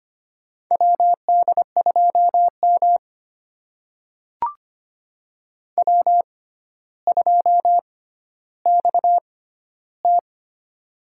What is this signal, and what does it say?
Morse code, heard as a keyed tone, about 25 wpm, spaced as for 13 wpm, 700 Hz. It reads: WD2M W2XT
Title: USA Call Signs - 4 Characters - 6x Character Spacing - 25wpm